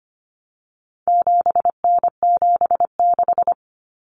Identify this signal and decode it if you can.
Morse code, heard as a keyed tone, about 25 wpm, 700 Hz. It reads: 7N76
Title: Common Cut Numbers - 4-Digit 25wpm